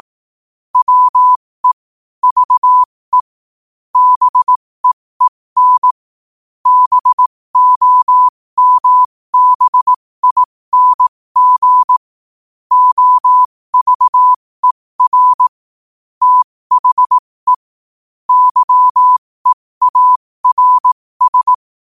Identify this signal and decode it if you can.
Morse code, heard as a keyed tone, about 18 wpm, spaced as for 15 wpm, 1000 Hz. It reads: WE VE BEEN BOMBING OVER THE YEARS
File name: qotdfiles / 15wpm